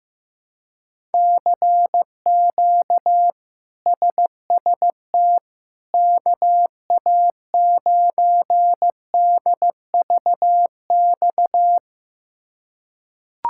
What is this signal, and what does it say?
Morse code, heard as a keyed tone, about 15 wpm, 700 Hz. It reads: CQ SST KA9DVX
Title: K1USN SST QSO Practice - 15wpm